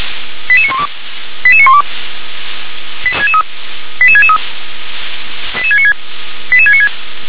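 Five-tone selective call (ZVEI), heard as quick runs of five tones, tones of about 70 ms. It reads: 88912 88912 88663 88696 88696
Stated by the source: Received by SDR, ZVEI five-tone call 88822.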